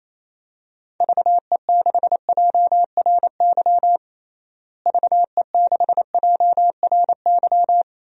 Morse code, heard as a keyed tone, about 28 wpm, 700 Hz. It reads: VE6JRY VE6JRY
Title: DXCC-100 Call Signs - 6 Characters - Repeated 3x - 28wpm